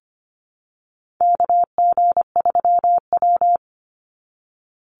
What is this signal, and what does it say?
Morse code, heard as a keyed tone, about 25 wpm, 700 Hz. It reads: KG3W
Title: NA SOTA Chaser Callsigns 25wpm